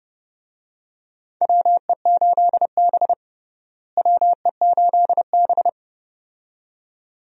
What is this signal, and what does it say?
Morse code, heard as a keyed tone, about 30 wpm, 700 Hz. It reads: WE8B WE8B
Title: USA Call Signs - 4 Characters - Repeated 2x - 30wpm